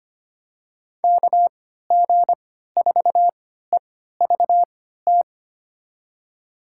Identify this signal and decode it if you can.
Morse code, heard as a keyed tone, about 25 wpm, spaced as for 14 wpm, 700 Hz. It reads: KG4EVT
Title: USA Call Signs - 6 Characters - 3x Character Spacing - 25wpm